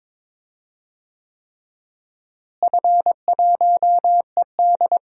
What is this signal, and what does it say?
Morse code, heard as a keyed tone, about 21 wpm, 700 Hz. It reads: F1ED
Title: DXCC-100 Call Signs - 4 Characters - Repeated 3x - 22wpm